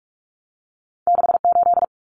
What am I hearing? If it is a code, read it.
Morse code, heard as a keyed tone, about 46 wpm, 700 Hz. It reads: 68
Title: Numbers - 2-Digit 45wpm